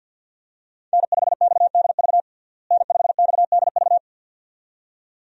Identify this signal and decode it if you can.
Morse code, heard as a keyed tone, about 52 wpm, 700 Hz. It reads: N5XDV N5XDV
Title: USA Call Signs - 5 Characters - Repeated 2x - 50wpm